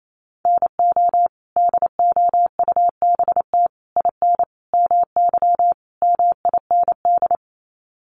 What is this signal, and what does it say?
Morse code, heard as a keyed tone, about 28 wpm, 700 Hz. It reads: NO DOUBT IN MY MIND
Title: Sets of 5 Words Encore2 28wpm